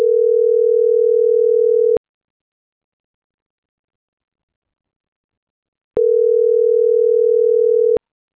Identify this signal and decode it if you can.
Ringback tone (North America), 440 + 480 Hz.